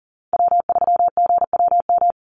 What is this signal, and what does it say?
Morse code, heard as a keyed tone, about 40 wpm, 700 Hz. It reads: W3GWM
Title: Call Signs Repeated 2x - 40wpm